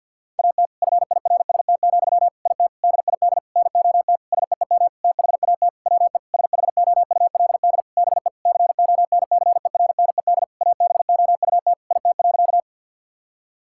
Morse code, heard as a keyed tone, about 51 wpm, 700 Hz. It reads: AT FIRST, IT DID NOT SEEM THAT WE SHOULD BE CONCERNED ABOUT IT.
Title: Sentences from Top 1000 Words - Review Part 6 - 50wpm